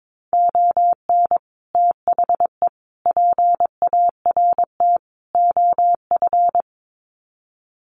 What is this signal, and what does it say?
Morse code, heard as a keyed tone, about 22 wpm, 700 Hz. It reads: ON THE PART OF
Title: Sets of 4 Words 22wpm